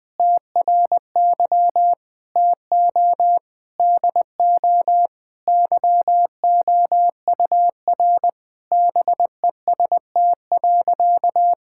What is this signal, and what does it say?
Morse code, heard as a keyed tone, about 21 wpm, 700 Hz. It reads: TRY TO DO YOUR BEST.
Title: Sentences with 2-Word Phrases - 20wpm